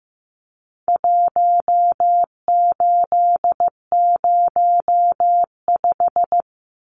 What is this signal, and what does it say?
Morse code, heard as a keyed tone, about 15 wpm, 700 Hz. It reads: 1805
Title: Numbers - Contest Style - 4-Digit 15wpm